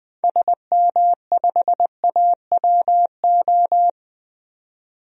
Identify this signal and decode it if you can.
Morse code, heard as a keyed tone, about 20 wpm, 700 Hz. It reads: SM5AWO
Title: DXCC-100 Call Signs - 6 Characters - 20wpm